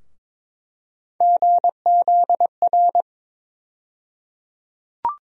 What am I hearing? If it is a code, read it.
Morse code, heard as a keyed tone, about 22 wpm, 700 Hz. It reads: GZR
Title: USA Call Sign Postfixes - 3 Letters 22wpm